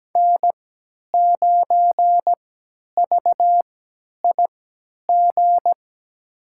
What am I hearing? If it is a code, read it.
Morse code, heard as a keyed tone, about 17 wpm, spaced as for 10 wpm, 700 Hz. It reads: N9VIG
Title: USA Call Signs - 3x Character Spacing - 17wpm